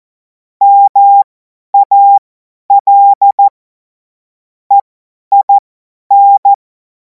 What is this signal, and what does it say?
Morse code, heard as a keyed tone, about 14 wpm, spaced as for 10 wpm, 800 Hz. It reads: MAL EIN
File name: Morsekurs / Witze / Witz 28. April 2026 50 F70